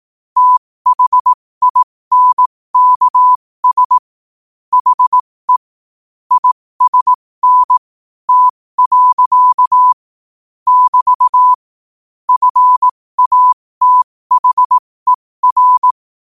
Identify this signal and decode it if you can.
Morse code, heard as a keyed tone, about 18 wpm, spaced as for 15 wpm, 1000 Hz. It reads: THINKS HE ISN T. <BT> FATHER
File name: qotdfiles / 15wpm